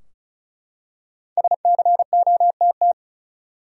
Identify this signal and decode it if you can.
Morse code, heard as a keyed tone, about 36 wpm, 700 Hz. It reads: SCOTT
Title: USA Common Names - Max 5 Letters - 35wpm